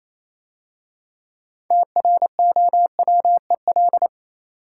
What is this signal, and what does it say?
Morse code, heard as a keyed tone, about 27 wpm, 690 Hz. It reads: TROWEL
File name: Easy Words - Speed-Racing with Hints - 22wpm